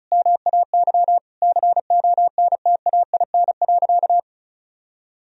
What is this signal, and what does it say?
Morse code, heard as a keyed tone, about 36 wpm, 700 Hz. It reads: MAY CONTAIN.
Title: Sentences from Top 1000 Words - Review Part 2 - 35wpm